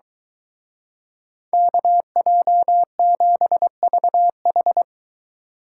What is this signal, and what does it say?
Morse code, heard as a keyed tone, about 23 wpm, 700 Hz. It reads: KJ7VH